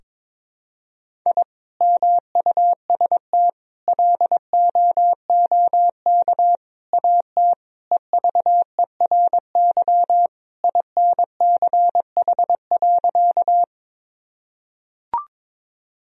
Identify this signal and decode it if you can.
Morse code, heard as a keyed tone, about 21 wpm, 700 Hz. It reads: I MUST LOOK AT EVERY INCH.
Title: Sentences from Top 1000 Words - Review Part 3 - 22wpm